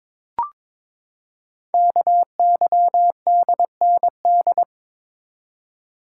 Morse code, heard as a keyed tone, about 22 wpm, 700 Hz. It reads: KYDND
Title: Code Groups - 5 Letters - 22wpm